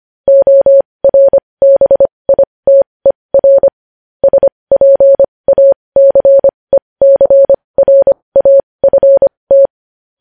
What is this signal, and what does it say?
Morse code, heard as a keyed tone, about 25 wpm, spaced as for 20 wpm, 550 Hz. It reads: ORBITER SPACECRAFT
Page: News Headlines in Morse Code at 20 WPM